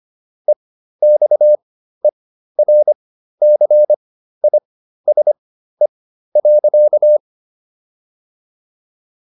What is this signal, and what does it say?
Morse code, heard as a keyed tone, about 25 wpm, spaced as for 13 wpm, 600 Hz. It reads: EXERCISE.